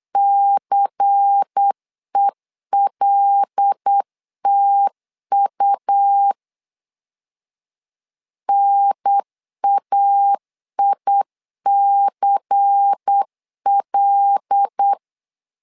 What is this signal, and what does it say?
Morse code, heard as a keyed tone, about 8 wpm, 800 Hz. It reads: CELTU NAICL
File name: LEZIONI / 5 LBVC / 5 LBVC 8wpm 2935